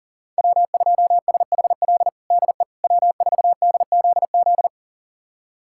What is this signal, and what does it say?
Morse code, heard as a keyed tone, about 37 wpm, 690 Hz. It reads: W2SHL DE W4DZZ
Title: USA Callsign DE Callsign - 5 Characters - 40wpm